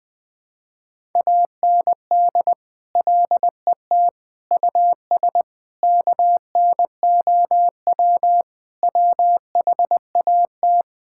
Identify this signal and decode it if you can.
Morse code, heard as a keyed tone, about 20 wpm, 700 Hz. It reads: AND LET US KNOW WHAT
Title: Sentences with 7-Word Phrases - 20wpm